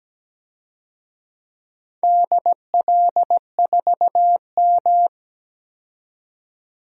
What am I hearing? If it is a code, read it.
Morse code, heard as a keyed tone, about 17 wpm, 700 Hz. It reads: DL4M
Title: DXCC-100 Call Signs - 4 Characters - 17wpm